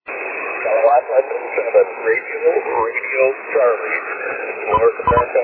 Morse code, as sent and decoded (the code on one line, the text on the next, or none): none